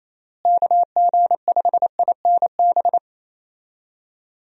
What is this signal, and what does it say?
Morse code, heard as a keyed tone, about 28 wpm, 700 Hz. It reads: KG5INB
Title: USA Call Signs - 6 Characters - Repeated 2x - 28wpm